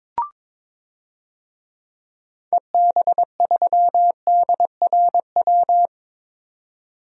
Morse code, heard as a keyed tone, about 21 wpm, 700 Hz. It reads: EB3DRW
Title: DXCC-100 Call Signs - 6 Characters - 22wpm